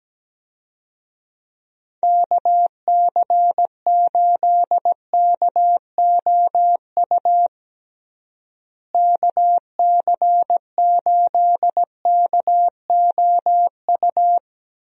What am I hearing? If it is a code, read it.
Morse code, heard as a keyed tone, about 17 wpm, 700 Hz. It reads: KC8KOU KC8KOU
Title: USA Call Signs - 6 Characters - Repeated 3x - 17wpm